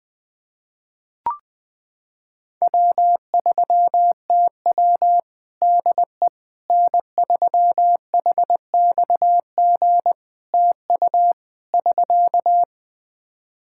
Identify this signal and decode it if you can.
Morse code, heard as a keyed tone, about 20 wpm, 700 Hz. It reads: W3TW DE N3HXG TU <SK>
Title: Practice QSO - Exchange 4 - Ending - 20wpm